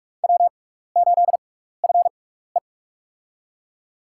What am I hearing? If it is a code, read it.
Morse code, heard as a keyed tone, about 45 wpm, spaced as for 15 wpm, 700 Hz. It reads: W8FE
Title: USA Call Signs - 4 Characters - 6x Character Spacing - 45wpm